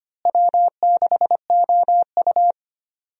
Morse code, heard as a keyed tone, about 25 wpm, 700 Hz. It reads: W6OU
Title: USA Call Signs - 4 Characters - Repeated 3x - 25wpm